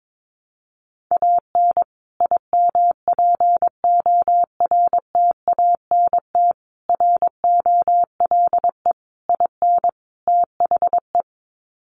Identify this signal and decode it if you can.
Morse code, heard as a keyed tone, about 22 wpm, 700 Hz. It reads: AN IMPORTANT ROLE IN THE
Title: Sets of 5 Words Encore2 22wpm